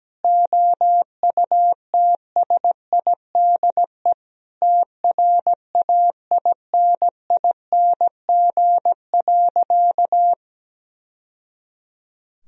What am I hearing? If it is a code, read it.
Morse code, heard as a keyed tone, about 17 wpm, 700 Hz. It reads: OUTSIDE TRAINING.
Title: Sentences from Top 1000 Words - Review Part 4 - 17wpm